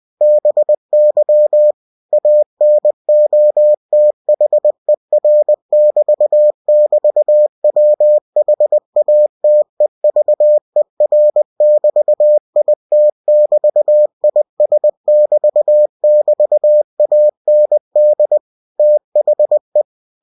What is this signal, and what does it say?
Morse code, heard as a keyed tone, about 20 wpm, 600 Hz. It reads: BY ANOTHER<BT><BT>WHATEVER<BT>IT<BT>IS<BT><BT>AND THE
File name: cw-books / winnie-the-pooh / winnie-20wpm-page-0022